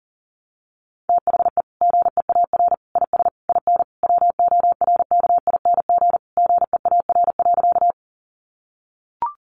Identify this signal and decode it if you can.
Morse code, heard as a keyed tone, about 40 wpm, 700 Hz. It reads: THE GEAR IS IN WORKING GEAR.
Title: Sentences from Top 1400 Words 40wpm